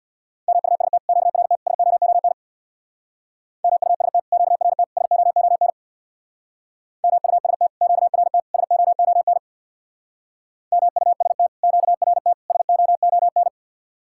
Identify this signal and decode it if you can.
Morse code, heard as a keyed tone, about 64 wpm, 700 Hz. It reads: MUST QRT SOON MUST QRT SOON MUST QRT SOON MUST QRT SOON